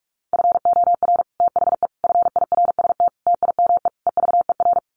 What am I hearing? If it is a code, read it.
Morse code, heard as a keyed tone, about 44 wpm, 700 Hz. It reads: FOR THE FIRST TIME EVER
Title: Sets of 5 Words Encore2 45wpm